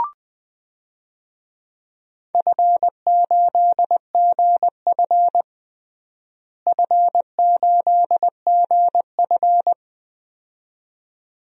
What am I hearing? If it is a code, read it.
Morse code, heard as a keyed tone, about 21 wpm, 700 Hz. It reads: F8GF F8GF
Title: DXCC-100 Call Signs - 4 Characters - Repeated 2x - 20wpm